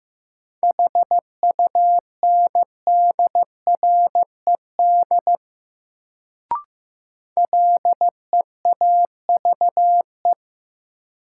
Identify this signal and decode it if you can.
Morse code, heard as a keyed tone, about 15 wpm, 700 Hz. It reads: HUNDRED LEAVE
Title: Sentences from Top 400 Words - New Word Drill - 15wpm